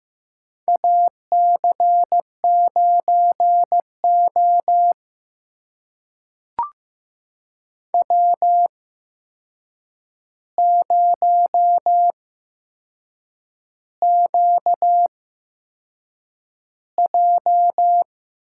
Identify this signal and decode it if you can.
Morse code, heard as a keyed tone, about 15 wpm, spaced as for 6 wpm, 700 Hz. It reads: AC9O W0QJ
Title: USA Call Signs - 4 Characters - 8x Character Spacing - 15wpm